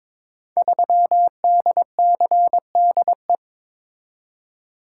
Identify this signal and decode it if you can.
Morse code, heard as a keyed tone, about 22 wpm, 700 Hz. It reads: 3DCDE
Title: Code Groups - 5 Letter:Numbers - 22wpm